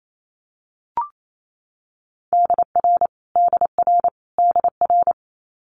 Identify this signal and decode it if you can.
Morse code, heard as a keyed tone, about 28 wpm, 700 Hz. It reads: DR DR DR
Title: Multi-letter Phonemes Repeated 3x 28wpm